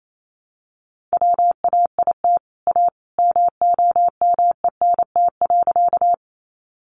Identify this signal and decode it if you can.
Morse code, heard as a keyed tone, about 28 wpm, 700 Hz. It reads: WAIT A MOMENT.
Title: Sentences from Top 1000 Words 28wpm